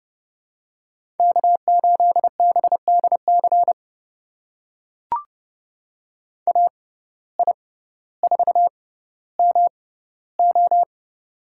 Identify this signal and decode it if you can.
Morse code, heard as a keyed tone, about 30 wpm, spaced as for 15 wpm, 700 Hz. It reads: K8BDC AI4MO
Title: USA Call Signs - 5 Characters - 6x Character Spacing - 30wpm